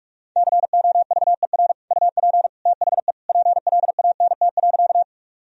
Morse code, heard as a keyed tone, about 46 wpm, 700 Hz. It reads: COVER UP THE PLANT.
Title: Sentences from Top 1000 Words - Review Part 1 - 45wpm